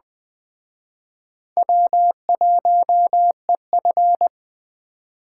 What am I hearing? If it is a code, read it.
Morse code, heard as a keyed tone, about 20 wpm, 700 Hz. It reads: W1EF
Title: Speed Racing - USA Call Signs 17wpm